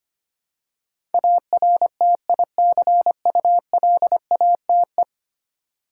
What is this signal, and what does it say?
Morse code, heard as a keyed tone, about 25 wpm, 700 Hz. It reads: ARTICULATE
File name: 10 Letter Words - First 3 Letters Given - 25wpm